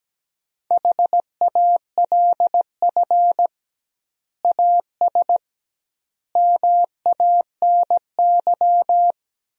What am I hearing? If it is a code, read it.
Morse code, heard as a keyed tone, about 17 wpm, 700 Hz. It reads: HALF AS MANY
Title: 2x Word-Spacing - Sentences from Top 400 Words - 17wpm